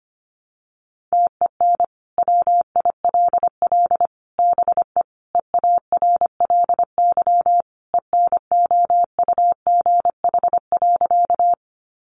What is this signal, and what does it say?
Morse code, heard as a keyed tone, about 25 wpm, 700 Hz. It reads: TEN WILL BE EARLY ENOUGH.
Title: Sentences from Top 400 Words 25wpm